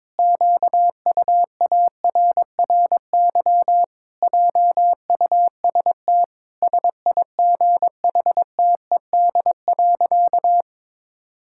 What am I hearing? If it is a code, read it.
Morse code, heard as a keyed tone, about 22 wpm, 700 Hz. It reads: QUARRY JUST SIGHTED.
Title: Pangram Challenge v2 - 22wpm